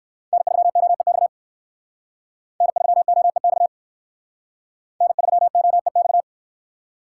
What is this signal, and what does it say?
Morse code, heard as a keyed tone, about 59 wpm, 700 Hz. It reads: N3OEX N3OEX N3OEX